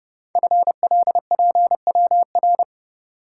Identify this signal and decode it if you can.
Morse code, heard as a keyed tone, about 30 wpm, 690 Hz. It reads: FLPWR
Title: Code Groups - 5 Letters - 30wpm